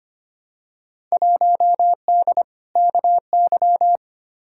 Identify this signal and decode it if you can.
Morse code, heard as a keyed tone, about 25 wpm, 700 Hz. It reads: 1D KY
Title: Field Day Exchanges - 25wpm